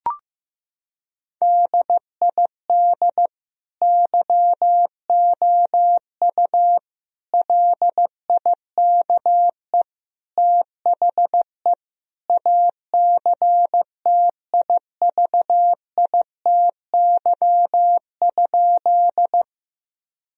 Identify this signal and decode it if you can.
Morse code, heard as a keyed tone, about 15 wpm, 700 Hz. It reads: DID YOU LIKE THE ACTIVITY?
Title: Sentences from Top 1000 Words - Review Part 4 - 15wpm